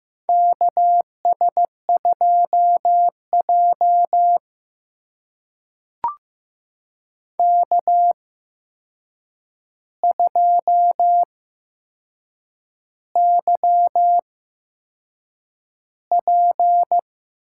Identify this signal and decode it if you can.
Morse code, heard as a keyed tone, about 15 wpm, spaced as for 6 wpm, 700 Hz. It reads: KS2J K2YP
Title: USA Call Signs - 4 Characters - 8x Character Spacing - 15wpm